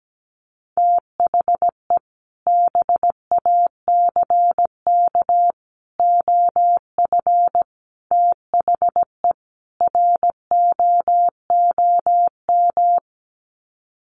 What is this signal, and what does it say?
Morse code, heard as a keyed tone, about 17 wpm, 700 Hz. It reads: THE BACK OF THE ROOM
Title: Sets of 5 Words Encore 17wpm